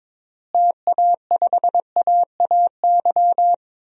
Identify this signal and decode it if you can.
Morse code, heard as a keyed tone, about 22 wpm, 700 Hz. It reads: TA5AAY
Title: DXCC-100 Call Signs - 6 Characters - Repeated 2x - 22wpm